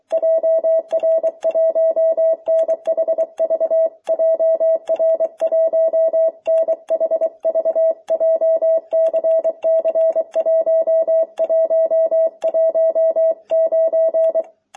Morse code, heard as a keyed tone, about 23 wpm, 650 Hz. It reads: JR1NHV JR1NHV JCC1119